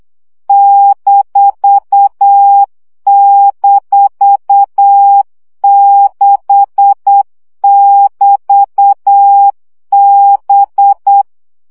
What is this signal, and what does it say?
Morse code, heard as a keyed tone, about 8 wpm, 800 Hz. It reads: --6<BT>B
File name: CW gyakorlas b 6 egyenlo minusz 7 baud